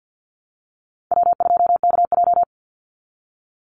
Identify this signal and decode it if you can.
Morse code, heard as a keyed tone, about 48 wpm, 700 Hz. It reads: W2KJ
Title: Easy Call Signs Repeated 3x - 50wpm